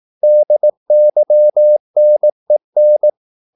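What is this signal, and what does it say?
Morse code, heard as a keyed tone, about 18 wpm, 600 Hz. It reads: DYNEN
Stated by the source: Slow (QRS) Morse practice at 18 wpm, CW med Gnister.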